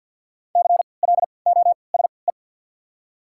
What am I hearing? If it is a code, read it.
Morse code, heard as a keyed tone, about 48 wpm, 700 Hz. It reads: C L O S E
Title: 3x Character Spacing - Top 300 Words - 50wpm